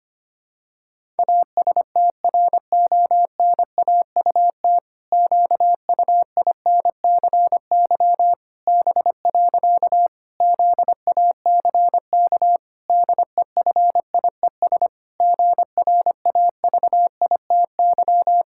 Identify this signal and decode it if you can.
Morse code, heard as a keyed tone, about 25 wpm, 700 Hz. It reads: ASTRONAUT QUINCY B. ZACK DEFIES GRAVITY
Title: Pangram Challenge v2 - 25wpm